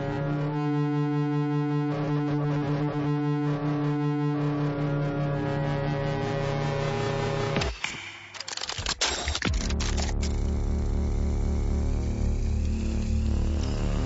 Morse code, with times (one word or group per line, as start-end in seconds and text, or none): none